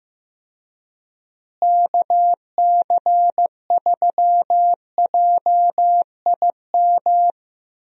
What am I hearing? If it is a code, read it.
Morse code, heard as a keyed tone, about 15 wpm, 700 Hz. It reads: KC3JIM
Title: USA Call Signs - 6 Characters - Repeated 3x - 15wpm